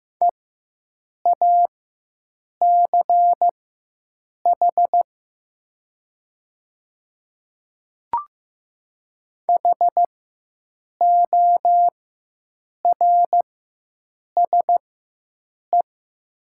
Morse code, heard as a keyed tone, about 15 wpm, spaced as for 7 wpm, 700 Hz. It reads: EACH HORSE
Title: 4x Character Spacing - Top 300 Words - 15wpm